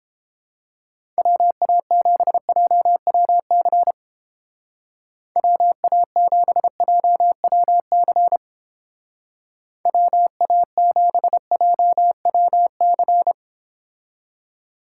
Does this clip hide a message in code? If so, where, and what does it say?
Yes: Morse code from 1.2 to 13.3 s, heard as a keyed tone, about 29 wpm, 700 Hz. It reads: WA7JWC WA7JWC WA7JWC